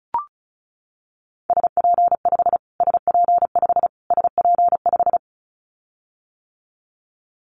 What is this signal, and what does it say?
Morse code, heard as a keyed tone, about 34 wpm, 700 Hz. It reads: SP5 SP5 SP5
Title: Callsign Prefixes for Top 25 DXCC - Repeated 3x - 35wpm